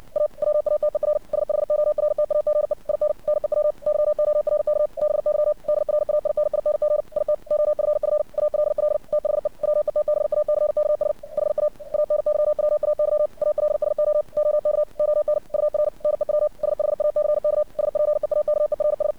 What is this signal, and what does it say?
Morse code, heard as a keyed tone, about 57 wpm, 600 Hz. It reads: A WATER SHORTAGE IN NEW YORK BY DRAINING IT OFF ALL THE WETBACKS HA ANYWAY ALSO OK ON UR NEW SHACK SPEAKERS
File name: archived / CRSnet